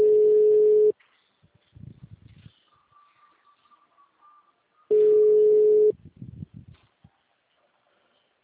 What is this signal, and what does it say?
Ringback tone (Europe, 425 Hz).
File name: out- -2035- - - .24